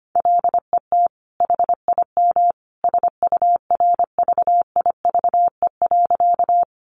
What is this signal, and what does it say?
Morse code, heard as a keyed tone, about 25 wpm, 700 Hz. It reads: LET HIM SURVIVE.
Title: Sentences from Top 1300 Words 25wpm